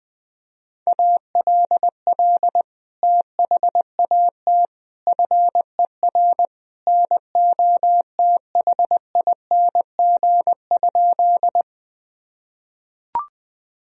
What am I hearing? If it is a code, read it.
Morse code, heard as a keyed tone, about 20 wpm, 700 Hz. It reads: ALL THAT FER NOTHING?
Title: Sentences from Top 500 Words and Abbreviations 20wpm